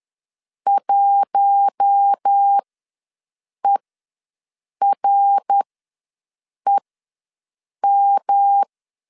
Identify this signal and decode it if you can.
Morse code, heard as a keyed tone, about 10 wpm, spaced as for 6 wpm, 800 Hz. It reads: 1EREM
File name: LEZIONI / 7 GF9Z / 7 G ws 10wpm 3843